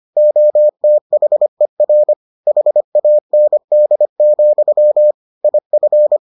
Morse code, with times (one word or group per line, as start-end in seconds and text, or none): 0.2-2.1 OTHER
2.5-5.1 HAND,
5.4-6.2 IF